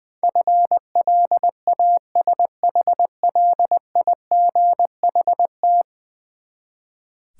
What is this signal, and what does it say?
Morse code, heard as a keyed tone, about 20 wpm, 700 Hz. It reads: FLASHLIGHT
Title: Easy Words with Hints - 20wpm